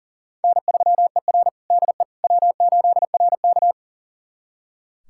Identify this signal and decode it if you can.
Morse code, heard as a keyed tone, about 40 wpm, 700 Hz. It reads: N3ER DE W8RK
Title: USA Callsign DE Callsign - 4 Characters - 40wpm